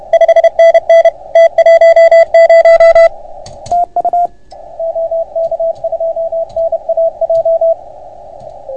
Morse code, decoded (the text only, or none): 5NN T10 TU OK2NAJ